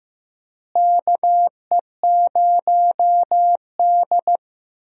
Morse code, heard as a keyed tone, about 15 wpm, 700 Hz. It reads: KE0D
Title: USA Call Signs - 4 Characters - Repeated 3x - 15wpm